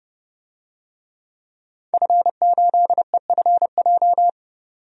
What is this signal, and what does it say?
Morse code, heard as a keyed tone, about 32 wpm, 700 Hz. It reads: F8EFJ
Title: DXCC-100 Call Signs - 5 Characters - 30wpm